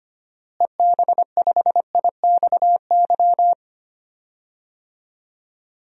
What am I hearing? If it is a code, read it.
Morse code, heard as a keyed tone, about 25 wpm, 700 Hz. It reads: EB5IXY
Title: DXCC-100 Call Signs - 6 Characters - Repeated 3x - 25wpm